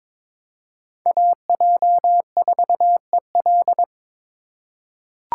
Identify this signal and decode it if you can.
Morse code, heard as a keyed tone, about 22 wpm, 700 Hz. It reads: AJ4EL
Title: USA Call Signs - 5 Characters - 4x Character Spacing - 22wpm